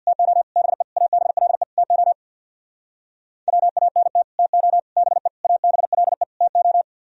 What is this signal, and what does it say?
Morse code, heard as a keyed tone, about 64 wpm, 700 Hz. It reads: TO BE ABLE TO WANT TO BE ABLE TO